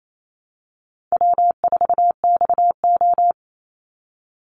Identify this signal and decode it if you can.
Morse code, heard as a keyed tone, about 28 wpm, 700 Hz. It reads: W4XO
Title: Easy Call Signs Repeated 3x - 28wpm